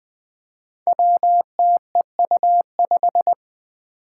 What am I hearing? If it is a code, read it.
Morse code, heard as a keyed tone, about 20 wpm, 700 Hz. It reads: WTEU5
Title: Five Challenging Letter-Number Combinations 20wpm